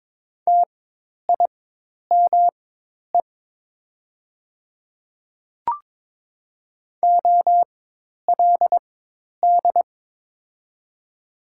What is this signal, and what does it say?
Morse code, heard as a keyed tone, about 22 wpm, spaced as for 10 wpm, 700 Hz. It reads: TIME OLD
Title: 4x Character Spacing - Top 200 Words - 22wpm